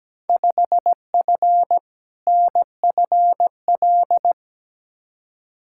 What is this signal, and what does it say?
Morse code, heard as a keyed tone, about 17 wpm, 700 Hz. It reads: 5F NFL
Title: Field Day Exchanges Encore - 17wpm